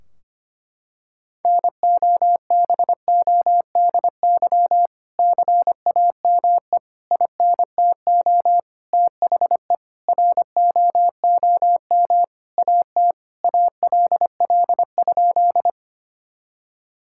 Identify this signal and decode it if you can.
Morse code, heard as a keyed tone, about 25 wpm, 700 Hz. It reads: NOBODY CAME INTO THE ROOM AT ALL?
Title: Sentences from Top 1600 Words 25wpm